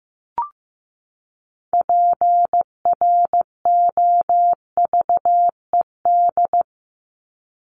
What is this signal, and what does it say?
Morse code, heard as a keyed tone, about 15 wpm, 700 Hz. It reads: PROVED
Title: ED Suffix - Top 100 Words with Base Words 15wpm